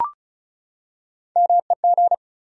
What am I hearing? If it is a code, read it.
Morse code, heard as a keyed tone, about 34 wpm, 700 Hz. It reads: MEG
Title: USA Common Names - Max 4 Letters - 35wpm